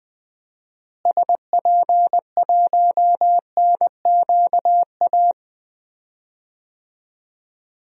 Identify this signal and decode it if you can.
Morse code, heard as a keyed tone, about 21 wpm, 700 Hz. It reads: SP1NQA
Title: DXCC-100 Call Signs - 6 Characters - Repeated 3x - 20wpm